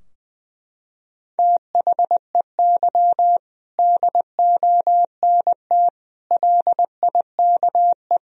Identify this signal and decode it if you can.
Morse code, heard as a keyed tone, about 20 wpm, 700 Hz. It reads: THEY DONT LIKE
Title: Sentences from Top 1000 Words - Review Part 2 - 20wpm